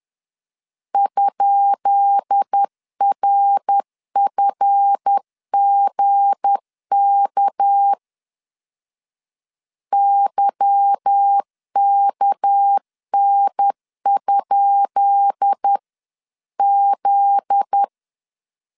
Morse code, heard as a keyed tone, about 10 wpm, 800 Hz. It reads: ?RFGK YKN? Z
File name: LEZIONI / 10 SIGNS / 10B INT 10wpm 2844